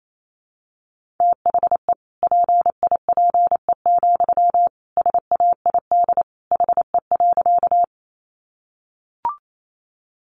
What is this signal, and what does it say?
Morse code, heard as a keyed tone, about 28 wpm, 700 Hz. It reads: THE PIPE, SAID HE.
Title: Sentences from Top 1500 Words 28wpm